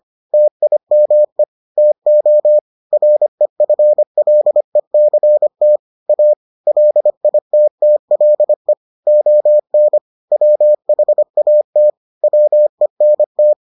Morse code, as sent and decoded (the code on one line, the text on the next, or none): - .. -- . / - --- / .-. ..-. .-.. -.-. - / .- / .-.. .. - - .-.. / --- -. / .-- .... .- - / .-- . -. -
TIME TO REFLECT A LITTLE ON WHAT WENT